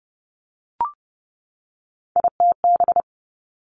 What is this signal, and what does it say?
Morse code, heard as a keyed tone, about 30 wpm, 700 Hz. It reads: ITB
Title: Three Challenging Letter-Number Combinations 30wpm